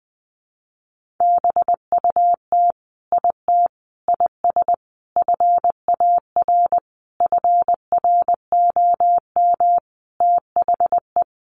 Morse code, heard as a keyed tone, about 20 wpm, 700 Hz. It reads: BUT IT IS FAR FROM THE
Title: Sentences from Top 1400 Words 20wpm